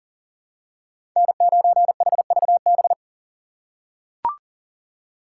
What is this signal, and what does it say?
Morse code, heard as a keyed tone, about 40 wpm, 700 Hz. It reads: N9HVB
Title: USA Call Signs - 5 Characters - 4x Character Spacing - 40wpm